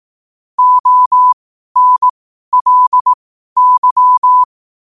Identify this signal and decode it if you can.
Morse code, heard as a keyed tone, about 18 wpm, spaced as for 12 wpm, 1000 Hz. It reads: ONLY
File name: qotdfiles / 12wpm